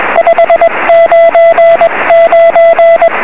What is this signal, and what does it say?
Morse code, heard as a keyed tone, about 21 wpm, 670 Hz, with noise. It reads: 599
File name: 1651z sm3bei 23cm